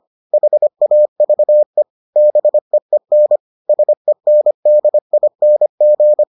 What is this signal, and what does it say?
Morse code, heard as a keyed tone, about 25 wpm, 600 Hz. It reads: HAVE BEEN SENDING